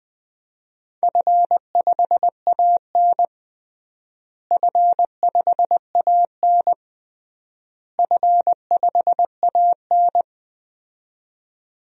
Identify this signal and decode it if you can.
Morse code, heard as a keyed tone, about 20 wpm, 700 Hz. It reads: F5AN F5AN F5AN
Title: DXCC-100 Call Signs - 4 Characters - Repeated 3x - 20wpm